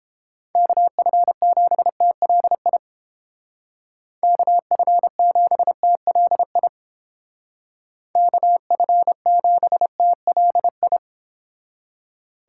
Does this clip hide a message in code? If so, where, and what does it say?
Yes: Morse code from 0.6 to 11.0 s, heard as a keyed tone, about 29 wpm, 700 Hz. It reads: KF7TLI KF7TLI KF7TLI